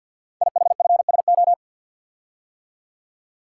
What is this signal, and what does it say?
Morse code, heard as a keyed tone, about 49 wpm, 700 Hz. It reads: IHFSO
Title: Code Groups - 5 Letters - 50wpm